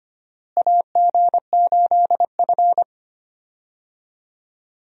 Morse code, heard as a keyed tone, about 25 wpm, 700 Hz. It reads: AG8F
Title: USA Call Signs - 4 Characters - Repeated 3x - 25wpm